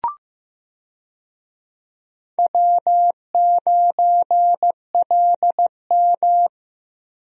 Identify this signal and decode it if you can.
Morse code, heard as a keyed tone, about 15 wpm, 700 Hz. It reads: W9LM
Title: USA Call Signs - 4 Characters - Repeated 2x - 15wpm